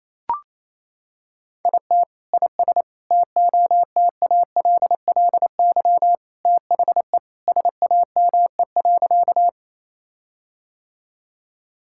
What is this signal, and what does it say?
Morse code, heard as a keyed tone, about 28 wpm, 700 Hz. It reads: IT IS TOTALLY THE SAME.
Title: Sentences from Top 1600 Words 28wpm